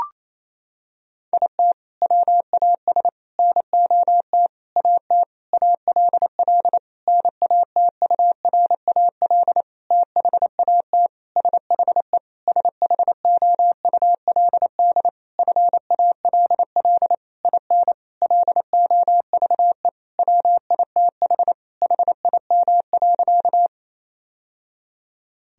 Morse code, heard as a keyed tone, about 28 wpm, 700 Hz. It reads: IT WAS NOT AT ALL NATURAL THAT SHE SHOULD FALL IN LOVE WITH HIM.
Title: Sentences from Top 1000 Words - Review Part 4 - 28wpm